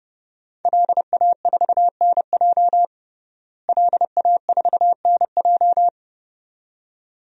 Morse code, heard as a keyed tone, about 32 wpm, 700 Hz. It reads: LA4NJ LA4NJ
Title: DXCC-100 Call Signs - 5 Characters - Repeated 2x - 30wpm